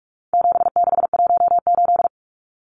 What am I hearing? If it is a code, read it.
Morse code, heard as a keyed tone, about 44 wpm, 700 Hz. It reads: 7618
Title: Numbers - 4-Digit 45wpm